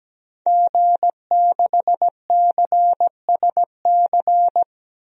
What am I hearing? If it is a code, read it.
Morse code, heard as a keyed tone, about 17 wpm, 700 Hz. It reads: G6CSC
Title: DXCC-100 Call Signs - 5 Characters - Repeated 3x - 17wpm